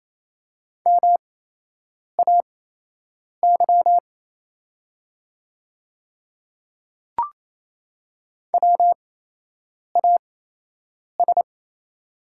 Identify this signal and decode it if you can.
Morse code, heard as a keyed tone, about 27 wpm, spaced as for 8 wpm, 700 Hz. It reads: MAY WAS
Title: 8x Character Spacing - Top 200 Words - 28wpm